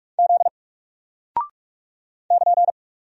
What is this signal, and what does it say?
Morse code, heard as a keyed tone, about 44 wpm, 700 Hz. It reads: Z <KN>
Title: Single Character (Letter:Number:Punctuation:Prosign) Alt 45wpm